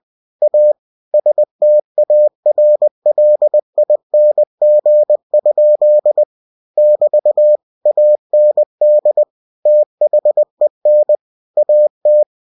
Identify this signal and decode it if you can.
Morse code, heard as a keyed tone, about 20 wpm, 600 Hz. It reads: A STARLING? <BT> AND THEN AT
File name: cw-books / winnie-the-pooh / winnie-20wpm-page-0069